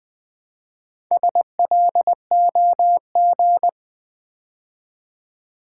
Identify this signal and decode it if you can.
Morse code, heard as a keyed tone, about 20 wpm, 700 Hz. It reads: SLOG
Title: Build A Word - 2 Letters At A Time - 6-Letter Words - 20wpm